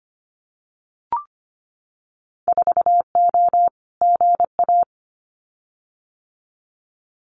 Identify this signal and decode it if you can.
Morse code, heard as a keyed tone, about 25 wpm, 700 Hz. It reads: 4O GA
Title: Winter Field Day Exchanges 25wpm